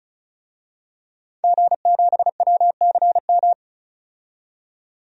Morse code, heard as a keyed tone, about 36 wpm, 700 Hz. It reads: G7WCM
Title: DXCC-100 Call Signs - 5 Characters - 35wpm